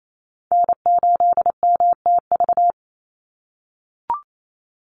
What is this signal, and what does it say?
Morse code, heard as a keyed tone, about 28 wpm, 700 Hz. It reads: N8MTV
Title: Call Signs - 28wpm